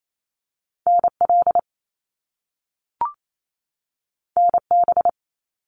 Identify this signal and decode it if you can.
Morse code, heard as a keyed tone, about 28 wpm, 700 Hz. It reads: NL NB
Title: Canadian Provinces and Territories 28wpm